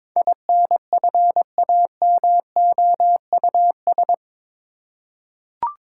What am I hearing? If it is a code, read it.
Morse code, heard as a keyed tone, about 22 wpm, 700 Hz. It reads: INFAMOUS
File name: practice / 22wpm / Eight Letter Words - 22wpm